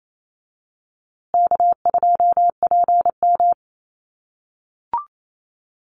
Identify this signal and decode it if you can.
Morse code, heard as a keyed tone, about 28 wpm, 700 Hz. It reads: K2PM
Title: USA Call Signs Repeated 3x 28wpm